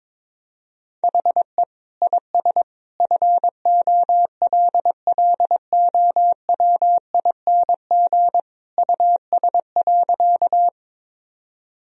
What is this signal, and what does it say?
Morse code, heard as a keyed tone, about 22 wpm, 700 Hz. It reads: HE IS FOLLOWING US.
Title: Sentences from Top 1000 Words - Review Part 4 - 22wpm